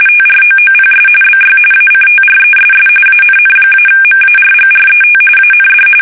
RTTY, about 45.45 baud. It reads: TROOPS BY ACTRESS ANN-MARGRET AND
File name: fsk850 / battleshipnj